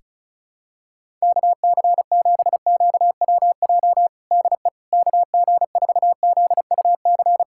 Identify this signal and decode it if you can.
Morse code, heard as a keyed tone, about 34 wpm, 700 Hz. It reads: KC7QWJ DE KG4ZUC
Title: USA Callsign DE Callsign - 6 Characters - 35wpm